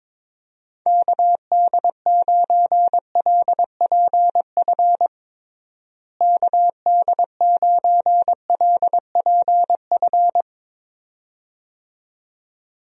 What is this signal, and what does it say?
Morse code, heard as a keyed tone, about 22 wpm, 700 Hz. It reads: KD9LPF KD9LPF
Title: USA Call Signs - 6 Characters - Repeated 2x - 22wpm